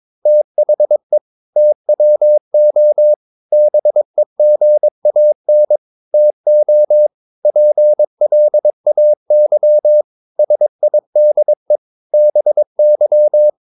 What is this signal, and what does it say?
Morse code, heard as a keyed tone, about 22 wpm, 600 Hz. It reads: THE TWO BEGAN TO PLAY SIDE BY
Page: Romanen Never af Ken Follet på engelsk.